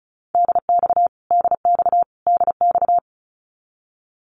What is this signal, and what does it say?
Morse code, heard as a keyed tone, about 34 wpm, 700 Hz. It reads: DX DX DX
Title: Abbreviations Repeated 3x - 35wpm